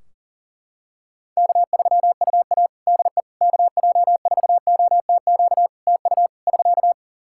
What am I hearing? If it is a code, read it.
Morse code, heard as a keyed tone, about 37 wpm, 700 Hz. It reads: K3UA DE KJ4OTQ TU <SK>
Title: Practice QSO - Exchange 4 - Ending - 40wpm